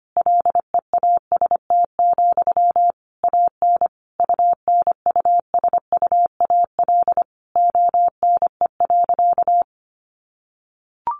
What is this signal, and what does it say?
Morse code, heard as a keyed tone, about 25 wpm, 700 Hz. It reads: LEAST, AN UNUSUAL ONE.
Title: Sentences from Top 1400 Words 25wpm